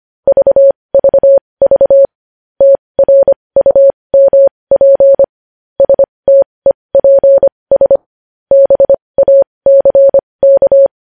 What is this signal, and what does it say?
Morse code, heard as a keyed tone, about 25 wpm, spaced as for 20 wpm, 550 Hz. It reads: VVV TRUMP STEPS BACK
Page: news headlines in morse code at 20 wpm